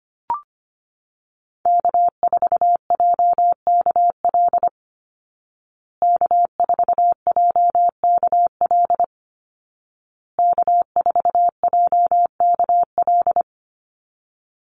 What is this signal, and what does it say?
Morse code, heard as a keyed tone, about 25 wpm, 700 Hz. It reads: K4JKL K4JKL K4JKL
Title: Call Signs Repeated 2x - 25wpm